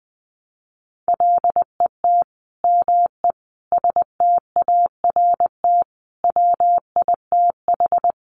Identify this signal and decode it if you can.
Morse code, heard as a keyed tone, about 20 wpm, 700 Hz. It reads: LET ME START WITH
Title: Sets of 5 Words Encore 20wpm